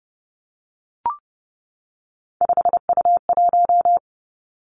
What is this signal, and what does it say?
Morse code, heard as a keyed tone, about 30 wpm, 700 Hz. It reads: 5U1
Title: Three Challenging Letter-Number Combinations 30wpm